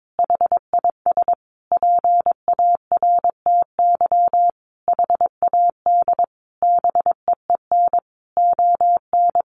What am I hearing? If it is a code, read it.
Morse code, heard as a keyed tone, about 22 wpm, 700 Hz. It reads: HIS PARTY HAD BEEN ON
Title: Sentences from Top 1400 Words 22wpm